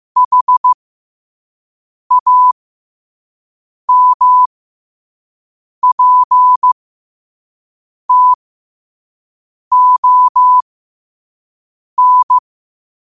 Morse code, heard as a keyed tone, about 15 wpm, spaced as for 5 wpm, 1000 Hz. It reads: HAMPTON
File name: qotdfiles / 5wpm